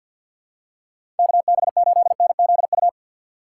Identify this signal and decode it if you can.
Morse code, heard as a keyed tone, about 52 wpm, 700 Hz. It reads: KB8NZU
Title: USA Call Signs - 6 Characters - 3x Character Spacing - 50wpm